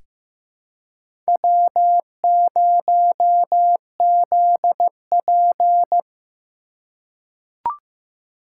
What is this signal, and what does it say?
Morse code, heard as a keyed tone, about 15 wpm, 700 Hz. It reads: W0ZP
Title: K1USN SST QSO Practice - 15wpm